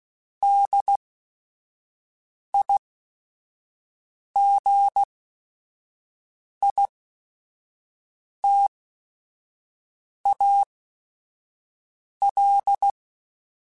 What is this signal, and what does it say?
Morse code, heard as a keyed tone, about 16 wpm, spaced as for 5 wpm, 800 Hz. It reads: DIGITAL